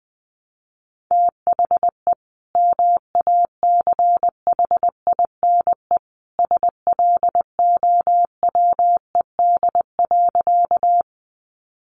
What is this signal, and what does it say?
Morse code, heard as a keyed tone, about 19 wpm, 700 Hz. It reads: THE MACHINE SLOWED.
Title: A Sound Of Thunder 20wpm